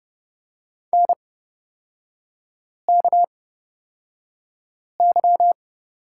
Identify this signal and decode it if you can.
Morse code, heard as a keyed tone, about 29 wpm, spaced as for 5 wpm, 700 Hz. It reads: NKY
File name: Lesson 37e - Letter x - Three Characters 30wpm